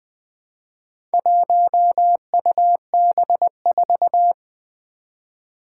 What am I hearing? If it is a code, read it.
Morse code, heard as a keyed tone, about 20 wpm, 700 Hz. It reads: 1UB4
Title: Four Challenging Letter-Number Combinations 20wpm